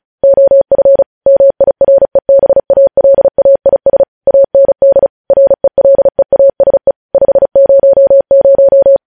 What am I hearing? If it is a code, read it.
Morse code, heard as a keyed tone, about 36 wpm, 550 Hz. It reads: OF MIREBALAIS AND RELEASE 500
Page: news headlines in morse code at 35 wpm